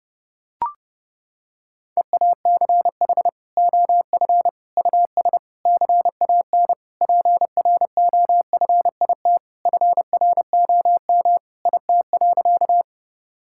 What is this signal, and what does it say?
Morse code, heard as a keyed tone, about 30 wpm, 700 Hz. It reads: EACH OF US CAN PROFIT FROM IT.
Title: Sentences from Top 1000 Words - Review Part 5 - 30wpm